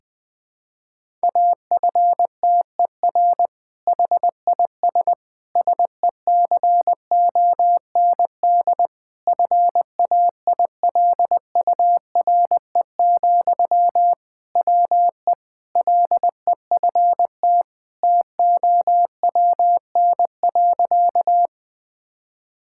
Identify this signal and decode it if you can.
Morse code, heard as a keyed tone, about 20 wpm, 700 Hz. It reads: AFTER HIS SECOND FAILURE, WE LEFT TOWN.
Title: Sentences from Top 1000 Words - Review Part 6 - 20wpm